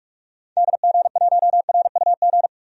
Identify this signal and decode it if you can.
Morse code, heard as a keyed tone, about 46 wpm, 700 Hz. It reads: DG1RUG
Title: DXCC-100 Call Signs - 6 Characters - 45wpm